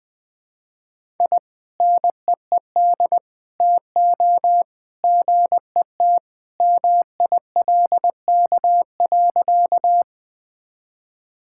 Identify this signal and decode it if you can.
Morse code, heard as a keyed tone, about 20 wpm, 700 Hz. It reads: I NEED TO GET MILK.
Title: Sentences with 2-Word Phrases - 20wpm